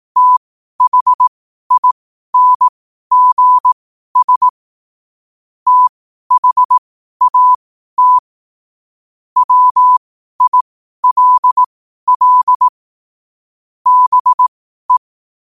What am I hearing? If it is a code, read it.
Morse code, heard as a keyed tone, about 18 wpm, spaced as for 12 wpm, 1000 Hz. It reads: THINGS THAT WILL BE
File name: qotdfiles / 12wpm